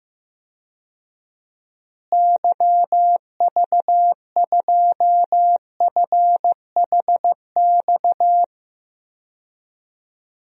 Morse code, heard as a keyed tone, about 15 wpm, 700 Hz. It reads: YV2FHX
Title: DXCC-100 Call Signs - 6 Characters - Repeated 2x - 15wpm